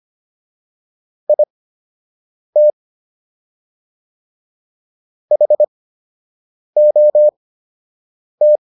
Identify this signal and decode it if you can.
Morse code, heard as a keyed tone, about 25 wpm, spaced as for 7 wpm, 600 Hz. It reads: IT HOT